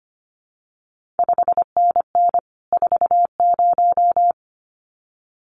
Morse code, heard as a keyed tone, about 25 wpm, 700 Hz. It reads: 5NN 40
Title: CQ WW DX Contest Exchange Encore - 25wpm